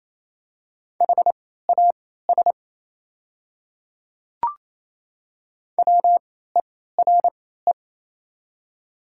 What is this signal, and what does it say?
Morse code, heard as a keyed tone, about 28 wpm, spaced as for 16 wpm, 700 Hz. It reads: HAS WERE
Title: 3x Character Spacing - Top 100 Words - 28wpm